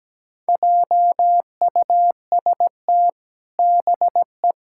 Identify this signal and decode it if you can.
Morse code, heard as a keyed tone, about 17 wpm, 700 Hz. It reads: JUST BE
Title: Sentences from Top 1600 Words 17wpm